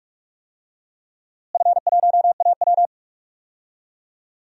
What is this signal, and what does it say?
Morse code, heard as a keyed tone, about 46 wpm, 700 Hz. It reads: F1AW
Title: DXCC-100 Call Signs - 4 Characters - Repeated 2x - 45wpm